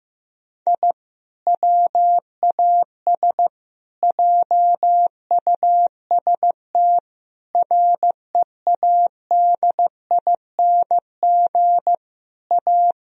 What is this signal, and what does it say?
Morse code, heard as a keyed tone, about 15 wpm, 700 Hz. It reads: I WAS JUST READING A
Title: Sentences from Top 1000 Words - Review Part 3 - 15wpm